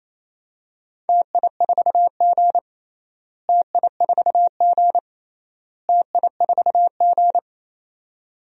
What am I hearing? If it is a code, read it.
Morse code, heard as a keyed tone, about 28 wpm, 700 Hz. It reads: TI4G TI4G TI4G
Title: DXCC-100 Call Signs - 4 Characters - Repeated 3x - 28wpm